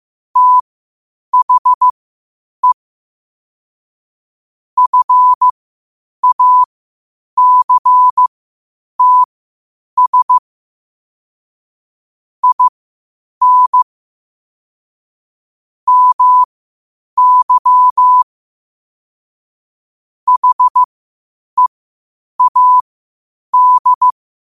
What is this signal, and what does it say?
Morse code, heard as a keyed tone, about 15 wpm, spaced as for 8 wpm, 1000 Hz. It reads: THE FACTS IN MY HEAD
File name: qotdfiles / 8wpm